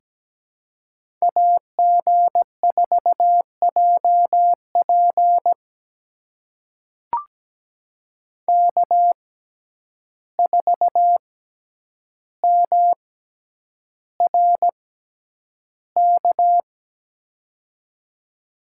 Morse code, heard as a keyed tone, about 17 wpm, spaced as for 9 wpm, 700 Hz. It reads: AG4JP K4MRK
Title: USA Call Signs - 5 Characters - 6x Character Spacing - 17wpm